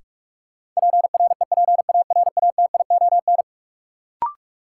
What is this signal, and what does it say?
Morse code, heard as a keyed tone, about 46 wpm, 700 Hz. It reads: PREPARATION
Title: Sentences from Top 1400 Words - New Word Drill Alt - 45wpm